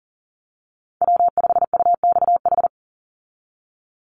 Morse code, heard as a keyed tone, about 40 wpm, 700 Hz. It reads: W5UXH
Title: Call Signs Repeated 2x - 40wpm